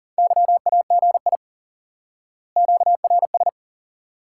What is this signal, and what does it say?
Morse code, heard as a keyed tone, about 40 wpm, 700 Hz. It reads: YAGI QRS
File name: IQE Course 3 - Lesson 89 - YAGI - Two Elements - 40wpm